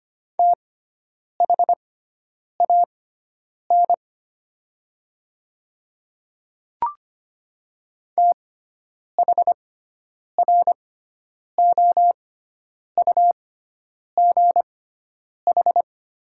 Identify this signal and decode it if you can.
Morse code, heard as a keyed tone, about 25 wpm, spaced as for 9 wpm, 700 Hz. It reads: THAN THROUGH
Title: 6x Character Spacing - Top 500 Words - 25wpm